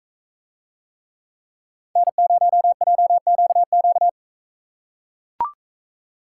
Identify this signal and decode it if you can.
Morse code, heard as a keyed tone, about 40 wpm, 700 Hz. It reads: N0JQQ